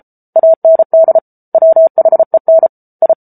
Morse code, heard as a keyed tone, about 34 wpm, 650 Hz. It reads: AND WHEN I